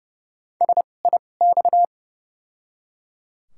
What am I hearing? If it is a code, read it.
Morse code, heard as a keyed tone, about 29 wpm, spaced as for 21 wpm, 700 Hz. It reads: SIX
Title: 2x Character Spacing - Top 500 Words - 30wpm